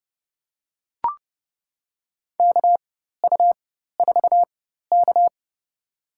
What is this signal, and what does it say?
Morse code, heard as a keyed tone, about 30 wpm, spaced as for 14 wpm, 700 Hz. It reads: KU4K
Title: USA Call Signs - 4 Characters - 4x Character Spacing - 30wpm